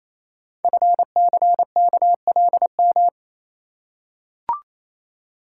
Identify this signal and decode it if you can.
Morse code, heard as a keyed tone, about 29 wpm, 700 Hz. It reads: FCKLM
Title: Code Groups - 5 Letters - 28wpm